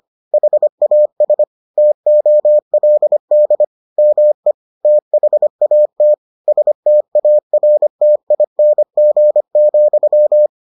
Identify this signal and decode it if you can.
Morse code, heard as a keyed tone, about 25 wpm, 600 Hz. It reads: HAS TOLD ME THAT STARTING,